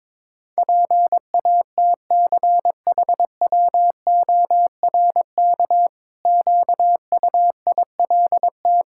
Morse code, heard as a keyed tone, about 22 wpm, 700 Hz. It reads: PATCHWORK QUILT
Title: Pangram Challenge v2 - 22wpm